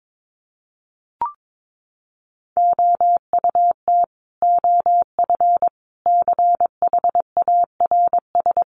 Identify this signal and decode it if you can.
Morse code, heard as a keyed tone, about 22 wpm, 700 Hz. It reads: OUT OF CHARS
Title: A Sound Of Thunder 22wpm